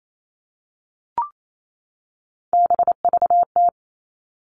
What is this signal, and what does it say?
Morse code, heard as a keyed tone, about 28 wpm, 700 Hz. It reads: BVT
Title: All Cut Numbers - 3-Digit 28wpm